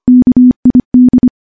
Morse code, heard as a keyed tone, about 25 wpm, 250 Hz. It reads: KID
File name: morse-training-25WPM